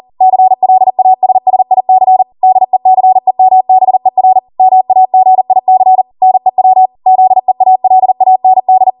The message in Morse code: CLASSIC DECEMBER MAGIC NEW ZEALAND